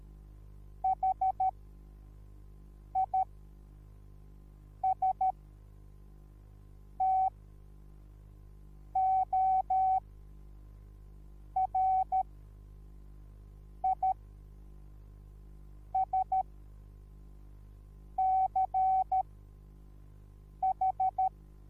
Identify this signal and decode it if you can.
Morse code, heard as a keyed tone, about 13 wpm, spaced as for 5 wpm, 760 Hz. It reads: HISTORISCH